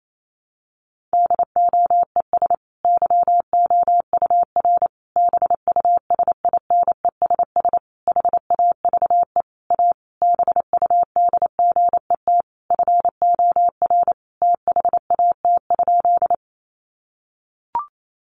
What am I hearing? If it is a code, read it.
Morse code, heard as a keyed tone, about 28 wpm, 700 Hz. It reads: DOES YOUR BUSINESS HAVE A BUDGET FOR THAT?
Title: Sentences from Top 1200 Words 28wpm